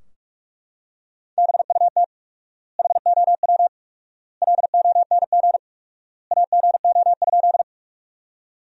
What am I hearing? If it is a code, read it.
Morse code, heard as a keyed tone, about 42 wpm, 700 Hz. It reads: BUT HOW LONG AGO?
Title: 4x Word-Spacing - Sentences from Top 500 Words - 45wpm